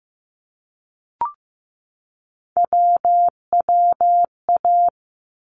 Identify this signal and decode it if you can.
Morse code, heard as a keyed tone, about 15 wpm, 700 Hz. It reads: WWA
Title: ARRL Sections 15wpm